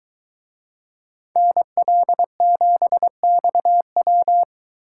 Morse code, heard as a keyed tone, about 23 wpm, 700 Hz. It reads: NL7XW